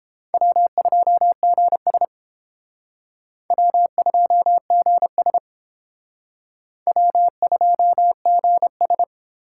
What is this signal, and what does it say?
Morse code, heard as a keyed tone, about 30 wpm, 700 Hz. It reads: W2GS W2GS W2GS